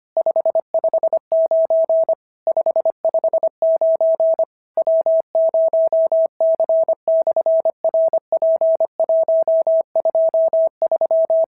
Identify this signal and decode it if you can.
Morse code, heard as a keyed tone, about 25 wpm, 650 Hz. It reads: 559 559 W0C/RP123
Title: SOTA QSO Practice 25wpm